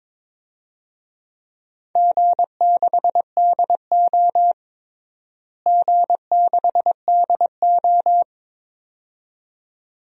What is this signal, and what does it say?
Morse code, heard as a keyed tone, about 21 wpm, 700 Hz. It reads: G6DO G6DO
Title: DXCC-100 Call Signs - 4 Characters - Repeated 2x - 22wpm